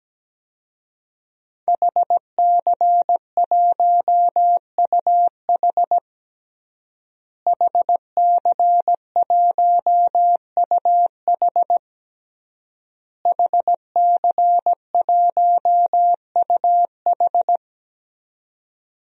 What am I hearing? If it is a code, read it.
Morse code, heard as a keyed tone, about 17 wpm, 700 Hz. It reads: HC1UH HC1UH HC1UH
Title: DXCC-100 Call Signs - 5 Characters - Repeated 3x - 17wpm